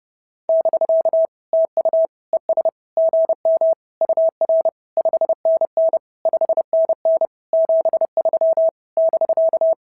Morse code, heard as a keyed tone, about 30 wpm, 650 Hz. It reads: <BK> TU ES GM UR 5NN 5NN 73 <BK>
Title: SOTA QSO Practice 30wpm